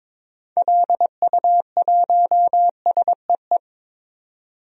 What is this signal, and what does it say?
Morse code, heard as a keyed tone, about 22 wpm, 700 Hz. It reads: LU1SEE
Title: DXCC-100 Call Signs - 6 Characters - 22wpm